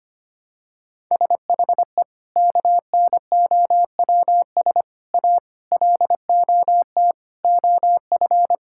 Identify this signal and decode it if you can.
Morse code, heard as a keyed tone, about 25 wpm, 700 Hz. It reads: SHE KNOWS A LOT OF
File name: practice / 25wpm / Sentences with 4-Word Phrases - 25wpm